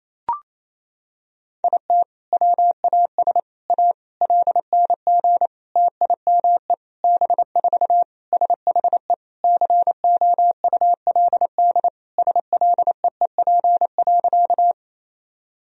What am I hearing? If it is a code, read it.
Morse code, heard as a keyed tone, about 28 wpm, 700 Hz. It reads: IT WAS A LNG TIME B4 SHE COULD SLEEP.
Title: Sentences from Top 500 Words and Abbreviations 28wpm